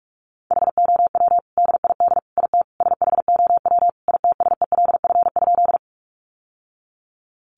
Morse code, heard as a keyed tone, about 44 wpm, 700 Hz. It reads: HOW DID IT SHOW ITSELF?
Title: Sentences from Top 800 Words 45wpm